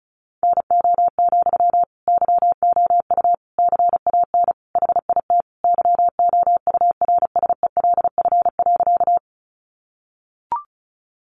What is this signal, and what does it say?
Morse code, heard as a keyed tone, about 34 wpm, 700 Hz. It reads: NO, YOU CAN HIT YOURSELF.
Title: Sentences from Top 1200 Words 35wpm